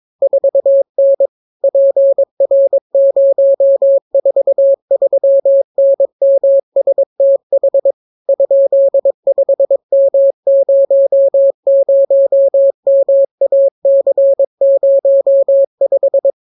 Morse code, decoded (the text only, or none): 4N PR043NMSTH ?5M00MAC05